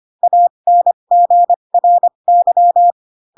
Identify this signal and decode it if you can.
Morse code, heard as a keyed tone, about 25 wpm, spaced as for 22 wpm, 700 Hz. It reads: ANGRY
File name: Pinocchio / Pinocchio ch6 18 25 ik1hge